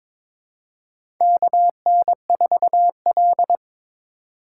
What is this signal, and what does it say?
Morse code, heard as a keyed tone, about 22 wpm, 700 Hz. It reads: KN4L
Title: USA Call Signs - 4 Characters - 6x Character Spacing - 22wpm